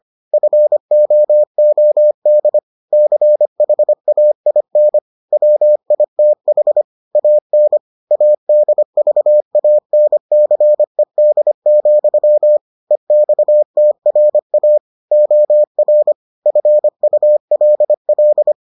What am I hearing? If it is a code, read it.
Morse code, heard as a keyed tone, about 25 wpm, 600 Hz. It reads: FOOD CHAIN WITH AN ADVANCED, EXTRA OR FULL